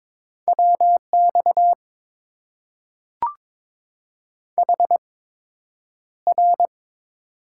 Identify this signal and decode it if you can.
Morse code, heard as a keyed tone, about 22 wpm, spaced as for 9 wpm, 700 Hz. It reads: WX HR
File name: Top 90 QSO Elements - 8x Character Spacing - 22wpm